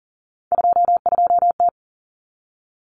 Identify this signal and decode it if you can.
Morse code, heard as a keyed tone, about 40 wpm, 700 Hz. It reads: 22T
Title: Common Cut Numbers - 3-Digit 40wpm